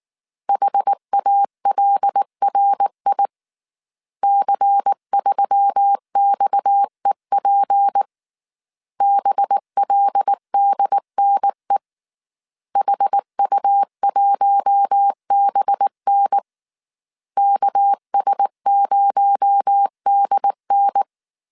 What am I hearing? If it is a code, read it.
Morse code, heard as a keyed tone, about 19 wpm, 800 Hz. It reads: HALRI C3XEP BLDNE HU1BN KS0DN
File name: CW / DET / 1 6 1500C S A 18wpm 2255